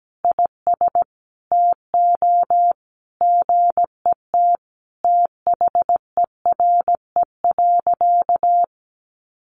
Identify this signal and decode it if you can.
Morse code, heard as a keyed tone, about 17 wpm, 700 Hz. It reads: IS TO GET THERE.
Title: Sentences from Top 1500 Words 17wpm